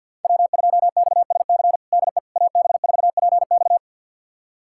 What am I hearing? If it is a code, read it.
Morse code, heard as a keyed tone, about 51 wpm, 700 Hz. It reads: W2XSC DE AB4PX
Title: USA Callsign DE Callsign - 5 Characters - 50wpm